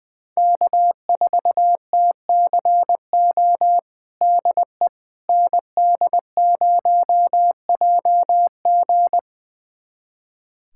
Morse code, heard as a keyed tone, about 20 wpm, 700 Hz. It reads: K4TCO DE ND0JG
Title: USA Callsign DE Callsign - 5 Characters - 20wpm